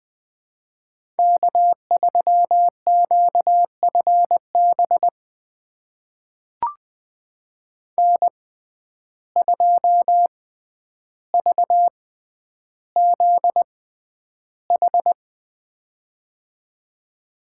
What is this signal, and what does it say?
Morse code, heard as a keyed tone, about 20 wpm, spaced as for 10 wpm, 700 Hz. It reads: K3QFB N2VZH
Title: USA Call Signs - 5 Characters - 6x Character Spacing - 20wpm